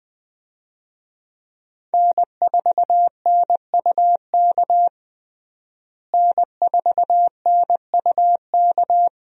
Morse code, heard as a keyed tone, about 20 wpm, 700 Hz. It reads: N4NUK N4NUK
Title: USA Call Signs - 5 Characters - Repeated 2x - 20wpm